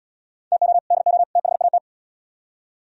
Morse code, heard as a keyed tone, about 74 wpm, 700 Hz. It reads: TO DO THAT